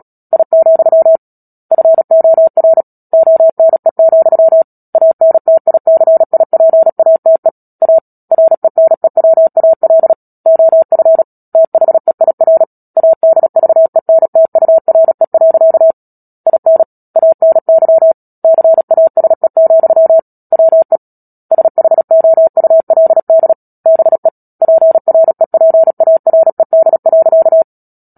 Morse code, heard as a keyed tone, about 37 wpm, 650 Hz. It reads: I, FOR ONE, ANTICIPATE A RENEWAL OF THEIR ADVENTURE. IN ANY CASE, WE SHOULD BE PREPARED.
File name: Book Two / 10-Chapter 10 - Epilogue - 36 WPM